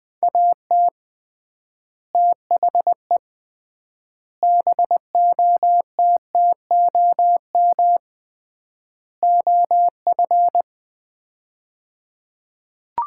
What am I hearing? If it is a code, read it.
Morse code, heard as a keyed tone, about 21 wpm, 700 Hz. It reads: AT THE BOTTOM OF
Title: 3x Word-Spacing - Sets of 4 Words - 20wpm